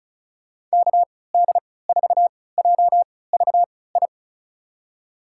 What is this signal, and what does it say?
Morse code, heard as a keyed tone, about 35 wpm, spaced as for 20 wpm, 700 Hz. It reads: KD4JVI
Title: USA Call Signs - 6 Characters - 3x Character Spacing - 35wpm